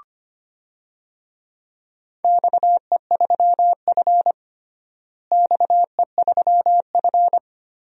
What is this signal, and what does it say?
Morse code, heard as a keyed tone, about 25 wpm, 700 Hz. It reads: XE3F XE3F
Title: DXCC-100 Call Signs - 4 Characters - Repeated 2x - 25wpm